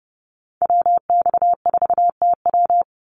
Morse code, heard as a keyed tone, about 30 wpm, 700 Hz. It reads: WX4TW
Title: NA SOTA Activator Callsigns 30wpm